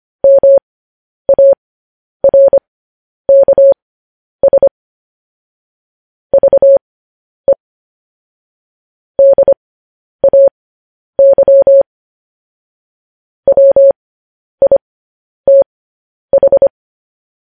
Morse code, heard as a keyed tone, about 25 wpm, spaced as for 10 wpm, 550 Hz. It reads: MARKS VE DAY WITH
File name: podcast / News headlines-1x-25x10